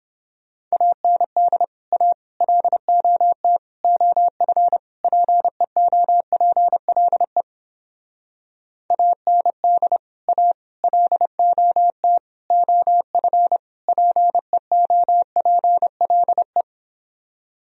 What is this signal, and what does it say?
Morse code, heard as a keyed tone, about 29 wpm, 700 Hz. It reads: AND A LOT OF PEOPLE AND A LOT OF PEOPLE